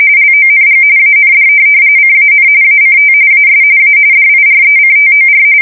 RTTY, about 45.45 baud. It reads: LONG AND CAN ACT AS A DIRECTOR
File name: fsk170 / USS-Missouri